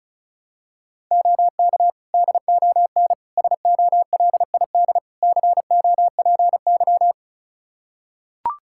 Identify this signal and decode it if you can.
Morse code, heard as a keyed tone, about 34 wpm, 700 Hz. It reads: OK DON SOLID COPY
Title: Practice QSO - Exchange 3 - More Info - 35wpm